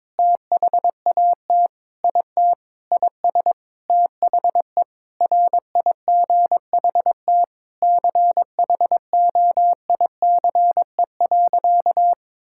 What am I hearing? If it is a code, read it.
Morse code, heard as a keyed tone, about 22 wpm, 700 Hz. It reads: THAT IT IS THE RIGHT CHOICE.
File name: practice / 22wpm / Sentences with 8-Word Phrases - 22wpm